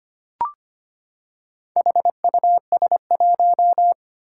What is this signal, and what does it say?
Morse code, heard as a keyed tone, about 25 wpm, 700 Hz. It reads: HUS1
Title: Four Challenging Letter-Number Combinations 25wpm